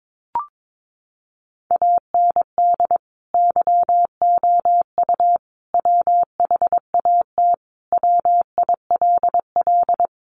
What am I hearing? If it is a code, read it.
Morse code, heard as a keyed tone, about 22 wpm, 700 Hz. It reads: AND YOU WHAT WILL
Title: Sentences from Top 600 Words 22wpm